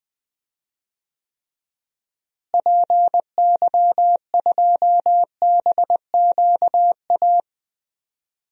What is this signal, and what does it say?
Morse code, heard as a keyed tone, about 20 wpm, 700 Hz. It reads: PY2BQA
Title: DXCC-100 Call Signs - 6 Characters - Repeated 3x - 20wpm